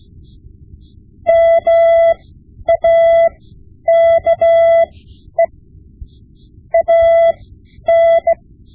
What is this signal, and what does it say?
Morse code, heard as a keyed tone, about 16 wpm, spaced as for 10 wpm, 670 Hz. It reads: MAKE AN
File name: Morse Lessons / Morse Lessons / Tape-5B